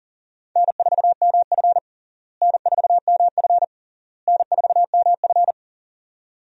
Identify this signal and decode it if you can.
Morse code, heard as a keyed tone, about 40 wpm, 700 Hz. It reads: N4MF N4MF N4MF
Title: USA Call Signs - 4 Characters - Repeated 3x - 40wpm